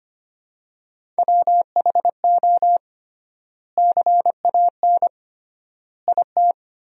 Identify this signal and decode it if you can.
Morse code, heard as a keyed tone, about 25 wpm, 700 Hz. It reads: WHO CAN IT
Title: 3x Word-Spacing - Sentences from Top 100 Words - 25wpm